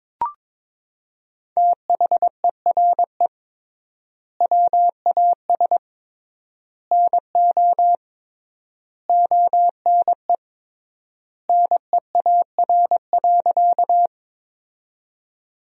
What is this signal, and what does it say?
Morse code, heard as a keyed tone, about 22 wpm, 700 Hz. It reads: THERE WAS NO ONE NEAR.
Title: 3x Word-Spacing - Sentences from Top 200 Words - 22wpm